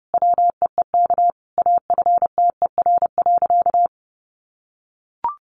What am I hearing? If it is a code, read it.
Morse code, heard as a keyed tone, about 30 wpm, 700 Hz. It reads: WEEK AFTER.
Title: Sentences from Top 500 Words 30wpm